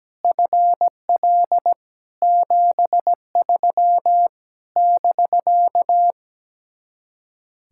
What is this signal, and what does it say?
Morse code, heard as a keyed tone, about 17 wpm, 700 Hz. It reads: FL 73 <BK>
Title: Speed-Racing Common QSO Phrases Encore - 17wpm